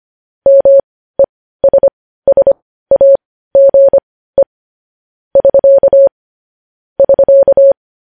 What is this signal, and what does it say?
Morse code, heard as a keyed tone, about 25 wpm, spaced as for 15 wpm, 550 Hz. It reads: MESSAGE <SK> <SK>
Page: news-headlines-in-morse-code-at-15-wpm